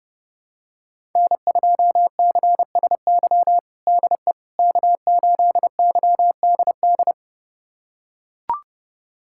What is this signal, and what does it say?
Morse code, heard as a keyed tone, about 30 wpm, 700 Hz. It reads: N2CSY DE K8YDD
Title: USA Callsign DE Callsign - 5 Characters - 30wpm